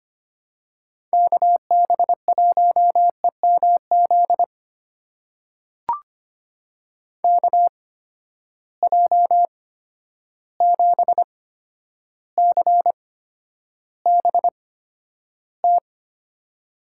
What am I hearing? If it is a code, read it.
Morse code, heard as a keyed tone, about 25 wpm, spaced as for 11 wpm, 700 Hz. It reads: KB1EMZ KJ7CBT
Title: USA Call Signs - 6 Characters - 8x Character Spacing - 25wpm